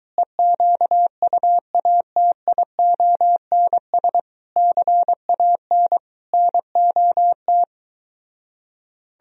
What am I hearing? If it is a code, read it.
Morse code, heard as a keyed tone, about 23 wpm, 700 Hz. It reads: EQUATIONS CAN NOT